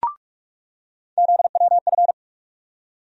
Morse code, heard as a keyed tone, about 44 wpm, 700 Hz. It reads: ZWF
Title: USA Call Sign Postfixes - 3 Letters 45wpm